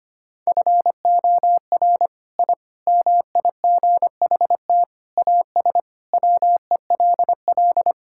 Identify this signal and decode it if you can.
Morse code, heard as a keyed tone, about 25 wpm, 700 Hz. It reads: FOR I MIGHT AS WELL
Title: Sentences from Top 1000 Words - Review Part 5 - 25wpm